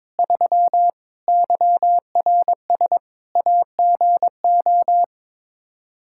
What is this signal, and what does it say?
Morse code, heard as a keyed tone, about 22 wpm, 700 Hz. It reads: 3 YRS AGO
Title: Practice QSO - Exchange 3 - More Info - 22wpm